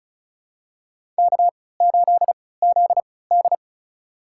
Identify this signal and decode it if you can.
Morse code, heard as a keyed tone, about 36 wpm, spaced as for 20 wpm, 700 Hz. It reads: K8ZD
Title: USA Call Signs - 4 Characters - 3x Character Spacing - 35wpm